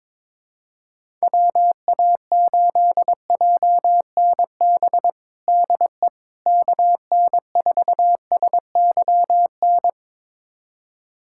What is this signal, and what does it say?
Morse code, heard as a keyed tone, about 22 wpm, 700 Hz. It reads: WA8JNB DE KN4SYN
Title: USA Callsign DE Callsign - 6 Characters - 22wpm